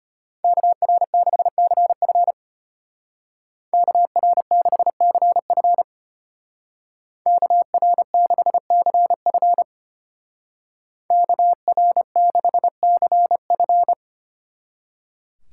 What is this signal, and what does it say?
Morse code, heard as a keyed tone, about 33 wpm, 700 Hz. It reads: KR6CF KR6CF KR6CF KR6CF